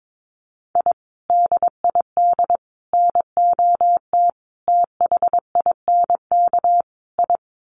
Morse code, heard as a keyed tone, about 22 wpm, 700 Hz. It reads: I DID NOT THINK I
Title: Sets of 5 Words Encore 22wpm